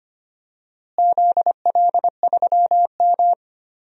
Morse code, heard as a keyed tone, about 25 wpm, 700 Hz. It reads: ZL3M
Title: DXCC-100 Call Signs - 4 Characters - Repeated 2x - 25wpm